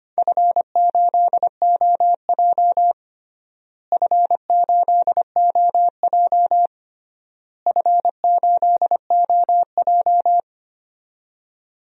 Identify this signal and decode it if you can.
Morse code, heard as a keyed tone, about 25 wpm, 700 Hz. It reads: F8OJ F8OJ F8OJ
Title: DXCC-100 Call Signs - 4 Characters - Repeated 3x - 25wpm